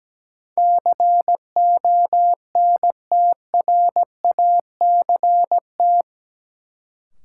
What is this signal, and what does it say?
Morse code, heard as a keyed tone, about 17 wpm, 700 Hz. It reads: CONTRACT
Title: Sentences from Top 1400 Words - New Word Drill - 17wpm